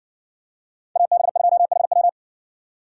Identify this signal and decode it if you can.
Morse code, heard as a keyed tone, about 60 wpm, 700 Hz. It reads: AB2HW